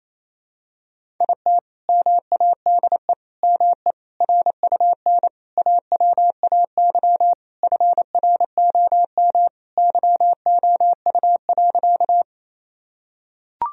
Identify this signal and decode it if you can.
Morse code, heard as a keyed tone, about 28 wpm, 700 Hz. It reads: IT MADE ME RUN AWAY FROM YOU.
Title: Sentences from Top 1000 Words - Review Part 1 - 28wpm